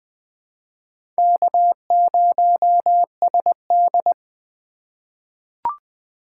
Morse code, heard as a keyed tone, about 20 wpm, 700 Hz. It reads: K0SD
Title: USA Call Signs - 4 Characters - 4x Character Spacing - 20wpm